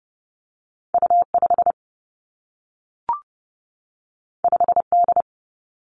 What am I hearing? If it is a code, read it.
Morse code, heard as a keyed tone, about 30 wpm, 700 Hz. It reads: U 5 5 D
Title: Two Challenging Letter-Number Combinations 30wpm